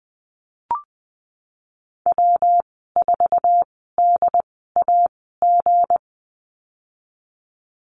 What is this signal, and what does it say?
Morse code, heard as a keyed tone, about 19 wpm, spaced as for 14 wpm, 700 Hz. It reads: W4DAG
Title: 193 Lesson 31g - Letter v - Call Signs 20wpm